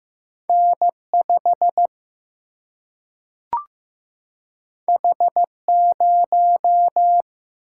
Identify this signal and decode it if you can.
Morse code, heard as a keyed tone, about 15 wpm, 700 Hz. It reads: N 5 H 0
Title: Single-Letter Single-Number 15wpm